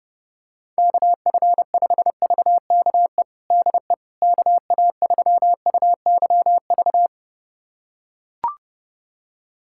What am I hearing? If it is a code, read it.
Morse code, heard as a keyed tone, about 30 wpm, 700 Hz. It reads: KF5VKE DE KA3UYV
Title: USA Callsign DE Callsign - 6 Characters - 30wpm